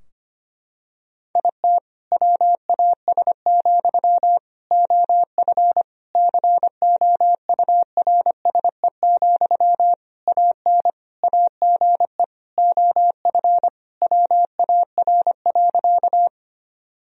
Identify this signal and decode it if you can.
Morse code, heard as a keyed tone, about 25 wpm, 700 Hz. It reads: IT WAS, OF COURSE, AN AGE OF WAR.
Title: Sentences from Top 1000 Words - Review Part 3 - 25wpm